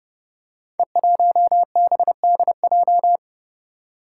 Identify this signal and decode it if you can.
Morse code, heard as a keyed tone, about 30 wpm, 700 Hz. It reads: E1BDJ
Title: Code Groups - 5 Letter:Numbers - 30wpm